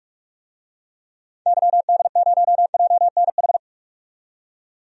Morse code, heard as a keyed tone, about 46 wpm, 700 Hz. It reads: YD0JNH
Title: DXCC-100 Call Signs - 6 Characters - 45wpm